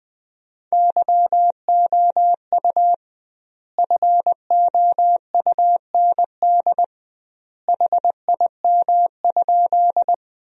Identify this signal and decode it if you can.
Morse code, heard as a keyed tone, about 21 wpm, 700 Hz. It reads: YOU FOUND HIM?
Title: 2x Word-Spacing - Sentences from Top 300 Words - 20wpm